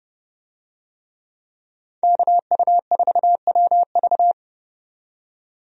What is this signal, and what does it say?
Morse code, heard as a keyed tone, about 29 wpm, 700 Hz. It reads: KU4WV
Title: USA Call Signs - 5 Characters - 30wpm